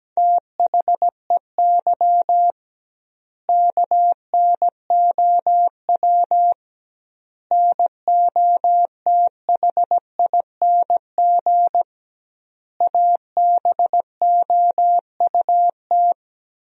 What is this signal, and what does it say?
Morse code, heard as a keyed tone, about 17 wpm, 700 Hz. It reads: THEY KNOW NOTHING ABOUT
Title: 2x Word-Spacing - Sentences from Top 500 Words - 17wpm